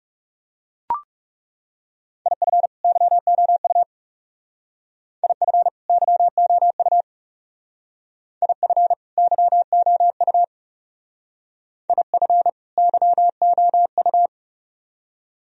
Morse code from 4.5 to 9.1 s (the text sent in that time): IF YOU IF